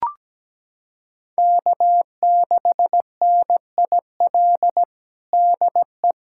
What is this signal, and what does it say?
Morse code, heard as a keyed tone, about 17 wpm, 700 Hz. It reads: K6NIL DE
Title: USA Callsign DE Callsign 17wpm